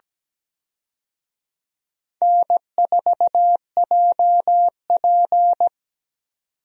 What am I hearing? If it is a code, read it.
Morse code, heard as a keyed tone, about 17 wpm, 700 Hz. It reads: N4JP
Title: USA Call Signs - 4 Characters - 17wpm